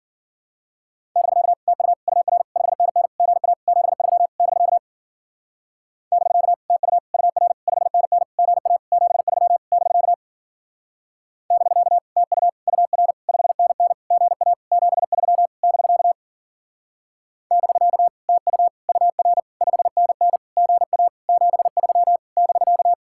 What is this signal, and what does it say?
Morse code, heard as a keyed tone, about 49 wpm, 700 Hz. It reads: <BK> TU UR 5NN GA 73 <BK> <BK> TU UR 5NN GA 73 <BK> <BK> TU UR 5NN GA 73 <BK> <BK> TU UR 5NN GA 73 <BK>
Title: Speed-Racing Common QSO Phrases - 40wpm